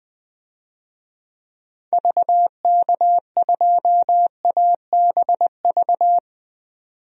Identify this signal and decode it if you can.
Morse code, heard as a keyed tone, about 20 wpm, 700 Hz. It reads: VK2ABV
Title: DXCC-100 Call Signs - 6 Characters - 20wpm